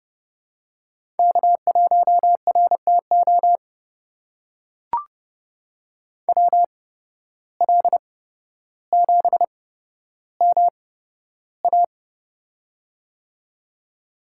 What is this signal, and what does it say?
Morse code, heard as a keyed tone, about 29 wpm, spaced as for 13 wpm, 700 Hz. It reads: K1RTO WL7MA
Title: USA Call Signs - 5 Characters - 8x Character Spacing - 30wpm